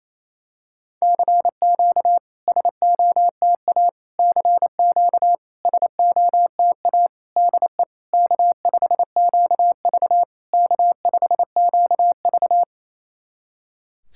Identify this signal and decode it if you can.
Morse code, heard as a keyed tone, about 28 wpm, 690 Hz. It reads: CQ SOTA CQ SOTA DE K5QV K5QV
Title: SOTA QSO Practice 28wpm